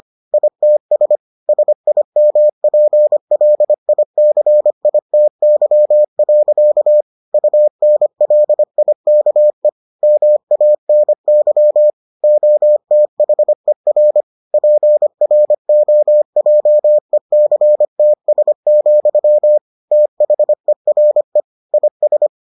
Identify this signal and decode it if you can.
Morse code, heard as a keyed tone, about 25 wpm, 600 Hz. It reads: ITS SIMPLICITY. UNLIKE MANY OTHER PROJECTS, THERE IS